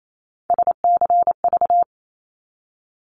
Morse code, heard as a keyed tone, about 28 wpm, 700 Hz. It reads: SCV
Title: ARRL Sections 28wpm